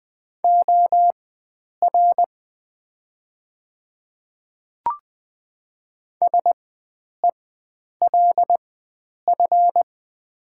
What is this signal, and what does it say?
Morse code, heard as a keyed tone, about 20 wpm, spaced as for 9 wpm, 700 Hz. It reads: OR SELF
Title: 4x Character Spacing - Top 500 Words - 20wpm